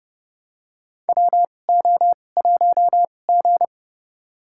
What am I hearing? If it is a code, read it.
Morse code, heard as a keyed tone, about 31 wpm, spaced as for 22 wpm, 700 Hz. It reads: WO1G
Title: USA Call Signs - 2x Character Spacing - 30wpm